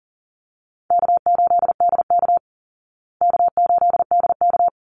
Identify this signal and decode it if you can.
Morse code, heard as a keyed tone, about 40 wpm, 700 Hz. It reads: K8DK K8DK
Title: Easy Call Signs Repeated 2x - 40wpm